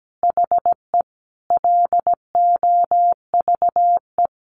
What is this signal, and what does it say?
Morse code, heard as a keyed tone, about 17 wpm, 700 Hz. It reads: HE LOVE
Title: Sentences from Top 400 Words 17wpm